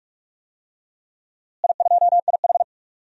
Morse code, heard as a keyed tone, about 46 wpm, 700 Hz. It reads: I2IH
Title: DXCC-100 Call Signs - 4 Characters - Repeated 2x - 45wpm